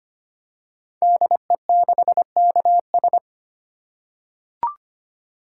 Morse code, heard as a keyed tone, about 25 wpm, 690 Hz. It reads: DE6KS
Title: Code Groups - 5 Letter:Numbers - 25wpm